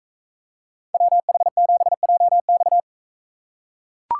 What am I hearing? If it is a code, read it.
Morse code, heard as a keyed tone, about 42 wpm, 700 Hz. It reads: WH7JX